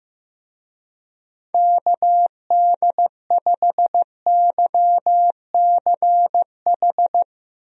Morse code, heard as a keyed tone, about 15 wpm, 700 Hz. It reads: KD5YCH
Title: USA Call Signs - 6 Characters - Repeated 3x - 15wpm